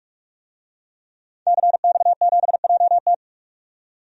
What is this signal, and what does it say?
Morse code, heard as a keyed tone, about 47 wpm, 700 Hz. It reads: CX7JT
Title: DXCC-100 Call Signs - 5 Characters - 45wpm